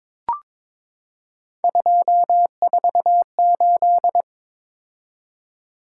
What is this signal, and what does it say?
Morse code, heard as a keyed tone, about 22 wpm, 700 Hz. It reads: 248
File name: practice / 22wpm / Three Letter-Number Combinations - 22wpm